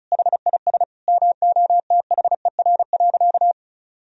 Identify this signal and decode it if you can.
Morse code, heard as a keyed tone, about 36 wpm, 700 Hz. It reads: HIS MOTHER.
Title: Sentences from Top 1000 Words - Review Part 4 - 35wpm